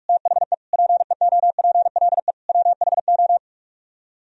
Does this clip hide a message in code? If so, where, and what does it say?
Yes: Morse code from 0.1 to 3.4 s, heard as a keyed tone, about 44 wpm, 700 Hz. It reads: THE PEOPLE WHO